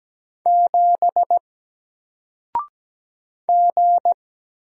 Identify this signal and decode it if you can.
Morse code, heard as a keyed tone, about 17 wpm, 700 Hz. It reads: 7 G
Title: Single Letter-Number 17wpm